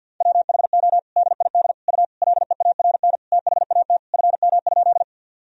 Morse code, heard as a keyed tone, about 50 wpm, 700 Hz. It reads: WHO DID U LEARN THAT FM?
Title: Sentences from Top 300 Words and Abbreviations 50wpm